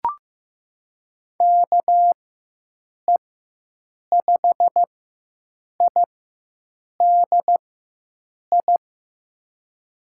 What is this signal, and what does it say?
Morse code, heard as a keyed tone, about 16 wpm, spaced as for 7 wpm, 700 Hz. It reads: KE5IDI
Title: USA Call Signs - 6 Characters - 4x Character Spacing - 15wpm